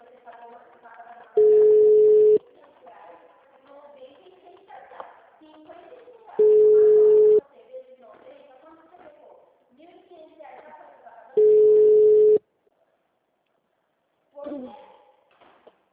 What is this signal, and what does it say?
Ringback tone (Europe, 425 Hz), with speech.
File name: out- -2015- - - .23391